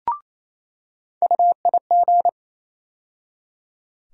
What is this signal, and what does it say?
Morse code, heard as a keyed tone, about 28 wpm, 700 Hz. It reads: UIG
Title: USA Call Sign Postfixes - 3 Letters 28wpm